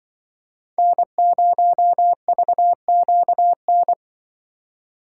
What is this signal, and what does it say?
Morse code, heard as a keyed tone, about 24 wpm, 700 Hz. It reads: N0VQN
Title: Speed Racing - USA 1x3 Call Signs 20wpm